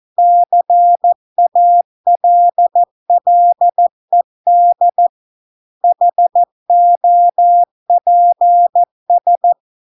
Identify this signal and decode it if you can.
Morse code, heard as a keyed tone, about 14 wpm, 700 Hz. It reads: CALLED HOPS